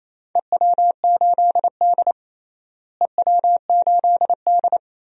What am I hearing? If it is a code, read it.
Morse code, heard as a keyed tone, about 28 wpm, 700 Hz. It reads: EW8D EW8D
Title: DXCC-100 Call Signs - 4 Characters - Repeated 2x - 28wpm